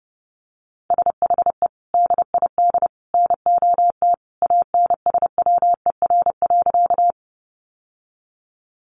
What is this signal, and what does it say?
Morse code, heard as a keyed tone, about 28 wpm, 700 Hz. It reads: SHE DID NOT ANSWER.
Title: Sentences from Top 600 Words 30wpm